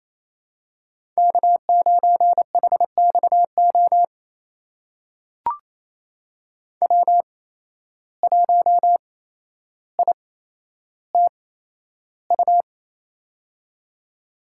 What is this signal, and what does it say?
Morse code, heard as a keyed tone, about 28 wpm, spaced as for 12 wpm, 700 Hz. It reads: K9HXO W1ITU
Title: USA Call Signs - 5 Characters - 8x Character Spacing - 28wpm